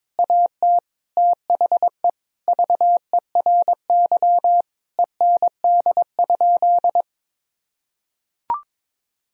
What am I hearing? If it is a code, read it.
Morse code, heard as a keyed tone, about 22 wpm, 700 Hz. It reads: AT THE VERY END?
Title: Sentences from Top 1000 Words - Review Part 3 - 22wpm